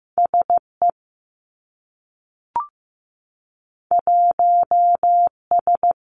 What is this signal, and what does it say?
Morse code, heard as a keyed tone, about 15 wpm, 700 Hz. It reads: SE 1S
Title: Two Challenging Letter-Number Combinations 15wpm